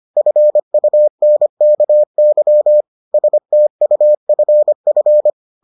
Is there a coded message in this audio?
Yes: Morse code from 0.2 to 5.3 s, heard as a keyed tone, about 25 wpm, 600 Hz. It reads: FUNKY STUFF